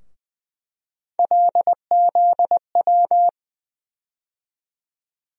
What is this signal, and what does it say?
Morse code, heard as a keyed tone, about 20 wpm, 700 Hz. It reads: LZW
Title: USA Call Sign Postfixes - 3 Letters 20wpm